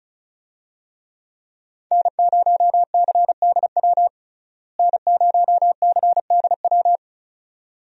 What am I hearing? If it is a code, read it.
Morse code, heard as a keyed tone, about 36 wpm, 700 Hz. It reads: N0CDW N0CDW
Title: USA Call Signs - 5 Characters - Repeated 2x - 35wpm